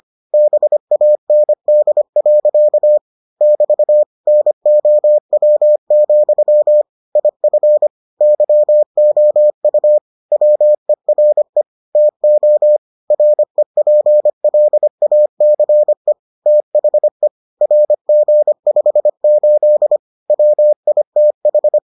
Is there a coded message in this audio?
Yes: Morse code from 0.3 to 21.8 s, heard as a keyed tone, about 25 wpm, 600 Hz. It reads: BAND. <BT> NOW, IF YOU WERE TO REPLACE THE RG58 WITH